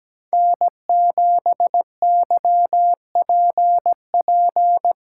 Morse code, heard as a keyed tone, about 19 wpm, 700 Hz. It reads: N7YPP
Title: Speed Racing - USA 1x3 Call Signs 17wpm